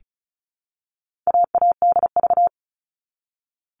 Morse code, heard as a keyed tone, about 34 wpm, 700 Hz. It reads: AADV
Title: All Cut Numbers - 4-Digit 35wpm